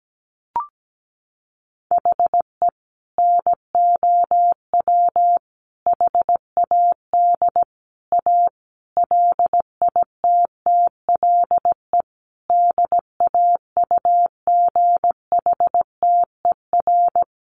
Morse code, heard as a keyed tone, about 17 wpm, 700 Hz. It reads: HE NOW HAD A LITTLE DAUGHTER
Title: Sentences from Top 1500 Words 17wpm